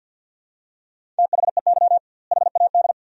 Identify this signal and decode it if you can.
Morse code, heard as a keyed tone, about 50 wpm, 700 Hz. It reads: THEY HAD
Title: 2x Word-Spacing - Sentences from Top 400 Words - 50wpm